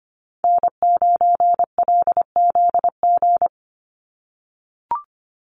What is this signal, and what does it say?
Morse code, heard as a keyed tone, about 25 wpm, 700 Hz. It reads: N9LZG
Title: USA Call Signs 25wpm